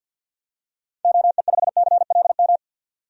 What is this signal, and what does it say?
Morse code, heard as a keyed tone, about 51 wpm, 700 Hz. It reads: OE5CLM
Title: DXCC-100 Call Signs - 6 Characters - 50wpm